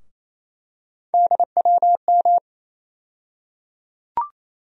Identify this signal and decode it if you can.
Morse code, heard as a keyed tone, about 28 wpm, 700 Hz. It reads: DWM
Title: USA Call Sign Postfixes - 3 Letters 28wpm